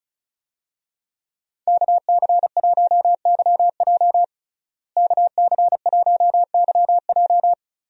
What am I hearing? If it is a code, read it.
Morse code, heard as a keyed tone, about 36 wpm, 700 Hz. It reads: KC1YJ KC1YJ
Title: USA Call Signs - 5 Characters - Repeated 3x - 35wpm